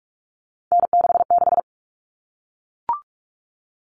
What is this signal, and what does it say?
Morse code, heard as a keyed tone, about 44 wpm, 700 Hz. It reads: N66
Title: All Cut Numbers - 3-Digit 45wpm